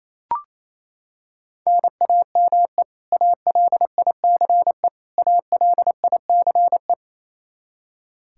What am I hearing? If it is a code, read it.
Morse code, heard as a keyed tone, about 28 wpm, 700 Hz. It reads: NAME ALICE ALICE
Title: Practice QSO - Exchange 1 - Beginning - 28wpm